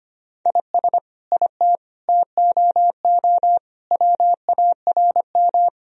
Morse code, heard as a keyed tone, about 25 wpm, 700 Hz. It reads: IS IT TOO WARM
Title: Sentences from Top 1000 Words - Review Part 2 - 25wpm